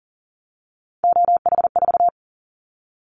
Morse code, heard as a keyed tone, about 42 wpm, 700 Hz. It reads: OH4
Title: Callsign Prefixes for Top 50 DXCC - 40wpm